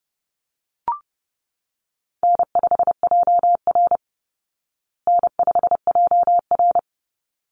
Morse code, heard as a keyed tone, about 30 wpm, 700 Hz. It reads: N5JR N5JR
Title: Call Signs Repeated 3x - 30wpm